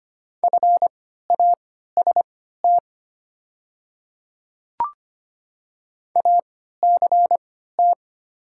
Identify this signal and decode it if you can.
Morse code, heard as a keyed tone, about 25 wpm, spaced as for 14 wpm, 700 Hz. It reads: FAST ACT
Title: 3x Character Spacing - Top 500 Words - 25wpm